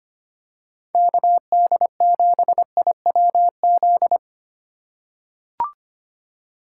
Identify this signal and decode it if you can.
Morse code, heard as a keyed tone, about 25 wpm, 700 Hz. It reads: KD7IWZ
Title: USA Call Signs - 6 Characters - 3x Character Spacing - 25wpm